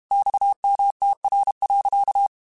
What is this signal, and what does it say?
Morse code, heard as a keyed tone, about 32 wpm, 800 Hz. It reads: XMTR.
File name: QSOs / QSOs-mp3s / WA7QOX 1@30